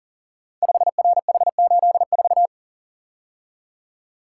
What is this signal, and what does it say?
Morse code, heard as a keyed tone, about 43 wpm, 700 Hz. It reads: 5RH84
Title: Code Groups - 5 Letter:Numbers - 40wpm